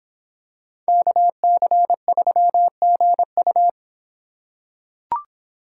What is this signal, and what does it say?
Morse code, heard as a keyed tone, about 25 wpm, 700 Hz. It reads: KC3GU